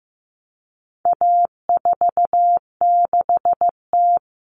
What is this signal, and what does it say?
Morse code, heard as a keyed tone, about 15 wpm, 700 Hz. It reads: A46T
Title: Common Cut Numbers - 4-Digit 15wpm